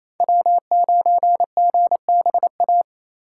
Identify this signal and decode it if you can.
Morse code, heard as a keyed tone, about 36 wpm, 700 Hz. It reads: W9GBA